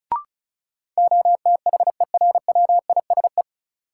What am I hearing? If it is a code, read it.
Morse code, heard as a keyed tone, about 34 wpm, 700 Hz. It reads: OTHERWISE
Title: Sentences from Top 1200 Words - New Word Drill - 35wpm